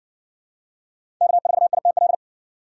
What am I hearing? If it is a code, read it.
Morse code, heard as a keyed tone, about 59 wpm, 700 Hz. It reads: K4ITL